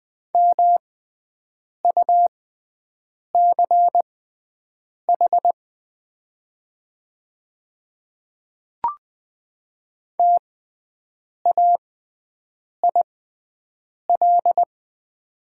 Morse code, heard as a keyed tone, about 20 wpm, spaced as for 7 wpm, 700 Hz. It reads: MUCH TAIL
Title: 6x Character Spacing - Top 500 Words - 20wpm